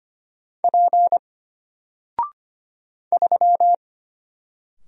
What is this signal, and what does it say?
Morse code, heard as a keyed tone, about 25 wpm, 700 Hz. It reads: P 3
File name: Single-Letter-Number-25wpm